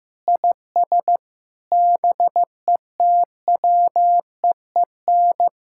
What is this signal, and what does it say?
Morse code, heard as a keyed tone, about 15 wpm, 700 Hz. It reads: IS BETWEEN
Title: Sentences from Top 1000 Words - Review Part 6 - 15wpm